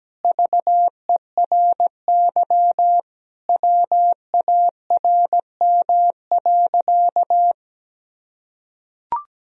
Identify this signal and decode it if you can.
Morse code, heard as a keyed tone, about 17 wpm, 700 Hz. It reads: VERY WARM.
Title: Sentences from Top 1000 Words - Review Part 2 - 17wpm